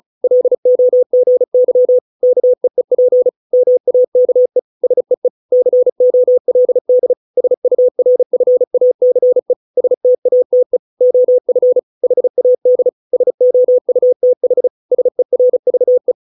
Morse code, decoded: LOGY KEEP MAKE SEE COLD SURFACE STATE OF HAD SOUTH SERVE

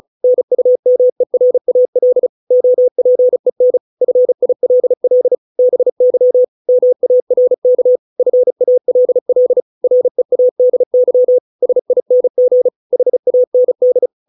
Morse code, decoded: NUMERAL OPEN FILL BY MARK FALL READY SING HAND

